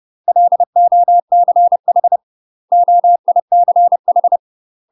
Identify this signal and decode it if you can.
Morse code, heard as a keyed tone, about 30 wpm, 700 Hz. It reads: LOCH OICH
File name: UK MASTER QSOs TRAINING / 30WPM / UK MASTER QSOs 0017